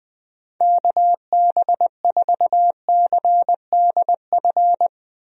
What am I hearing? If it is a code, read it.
Morse code, heard as a keyed tone, about 20 wpm, 700 Hz. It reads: KB4CDF
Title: USA Call Signs - 6 Characters - Repeated 2x - 20wpm